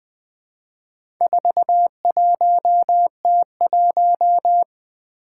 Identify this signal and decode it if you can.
Morse code, heard as a keyed tone, about 20 wpm, 700 Hz. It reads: 41T1
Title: Four Challenging Letter-Number Combinations 20wpm